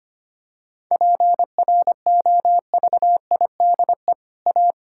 Morse code, heard as a keyed tone, about 25 wpm, 700 Hz. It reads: PROVIDE A
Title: Sentences from Top 1000 Words - Review Part 5 - 25wpm